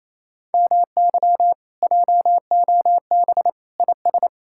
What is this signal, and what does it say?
Morse code, heard as a keyed tone, about 38 wpm, spaced as for 33 wpm, 700 Hz. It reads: MY JOB IS
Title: Speed-Racing Common QSO Phrases - 28wpm